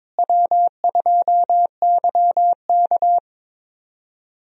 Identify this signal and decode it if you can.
Morse code, heard as a keyed tone, about 22 wpm, 700 Hz. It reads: W2YK
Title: USA Call Signs - 4 Characters - Repeated 2x - 22wpm